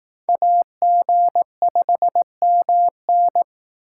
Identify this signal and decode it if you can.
Morse code, heard as a keyed tone, about 18 wpm, 700 Hz. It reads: AG5MN
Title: Speed Racing - USA 2x2 Call Signs 15wpm